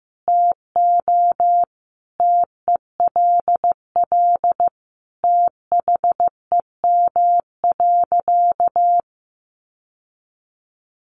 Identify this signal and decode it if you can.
Morse code, heard as a keyed tone, about 15 wpm, 700 Hz. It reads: TO TELL THEM.
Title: Sentences from Top 700 Words 15wpm